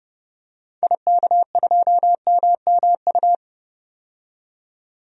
Morse code, heard as a keyed tone, about 30 wpm, 700 Hz. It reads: IK2MMU
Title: DXCC-100 Call Signs - 6 Characters - 30wpm